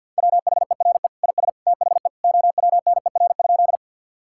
Morse code, heard as a keyed tone, about 50 wpm, 700 Hz. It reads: WHERE IS THE OWNER?
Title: Sentences from Top 1600 Words 50wpm